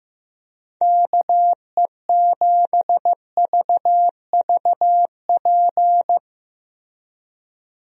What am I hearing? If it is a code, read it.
Morse code, heard as a keyed tone, about 15 wpm, 700 Hz. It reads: KE7VVP
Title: USA Call Signs - 6 Characters - Repeated 3x - 15wpm